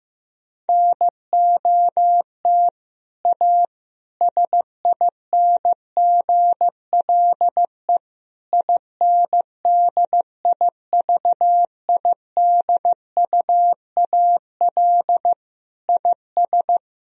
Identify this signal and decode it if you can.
Morse code, heard as a keyed tone, about 16 wpm, 700 Hz. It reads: NOT A SINGLE INDIVIDUAL IS
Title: Sentences from Top 1000 Words - Review Part 4 - 15wpm